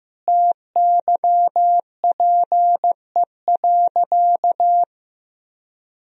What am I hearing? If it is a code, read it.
Morse code, heard as a keyed tone, about 15 wpm, 700 Hz. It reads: TYPE.
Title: Sentences from Top 1000 Words - Review Part 3 - 15wpm